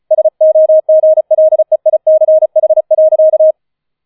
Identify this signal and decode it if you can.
Morse code, heard as a keyed tone, about 36 wpm, 630 Hz. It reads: SOGLEICH.